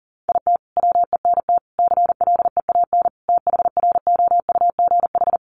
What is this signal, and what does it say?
Morse code, heard as a keyed tone, about 40 wpm, 700 Hz. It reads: IT WENT CLEAN THROUGH
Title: Sentences from Top 1200 Words 40wpm